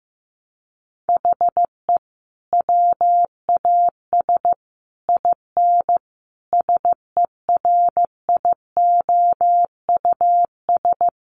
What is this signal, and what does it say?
Morse code, heard as a keyed tone, about 15 wpm, 700 Hz. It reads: HE WAS IN SERIOUS
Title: Sentences from Top 1300 Words 15wpm